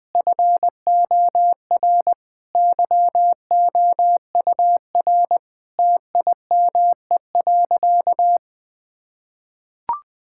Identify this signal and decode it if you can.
Morse code, heard as a keyed tone, about 20 wpm, 700 Hz. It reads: FOR YOUR TIME.
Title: Sentences with 5-Word Phrases - 20wpm